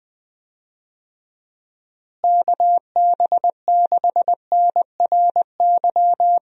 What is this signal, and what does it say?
Morse code, heard as a keyed tone, about 21 wpm, 700 Hz. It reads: KB6NRY
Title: USA Call Signs - 6 Characters - Repeated 3x - 20wpm